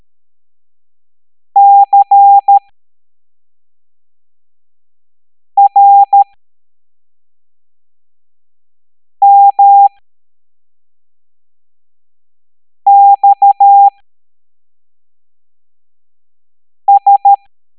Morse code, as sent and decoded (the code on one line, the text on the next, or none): -.-. .-. -- -..- ...
CRMXS